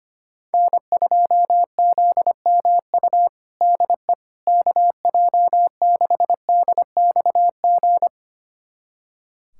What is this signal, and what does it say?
Morse code, heard as a keyed tone, about 25 wpm, 700 Hz. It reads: N2ZMU DE KJ6DXG
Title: USA Callsign DE Callsign 25wpm